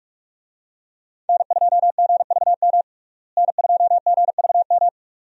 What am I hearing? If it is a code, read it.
Morse code, heard as a keyed tone, about 48 wpm, 700 Hz. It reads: N2GVM N2GVM
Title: USA Call Signs - 5 Characters - Repeated 2x - 45wpm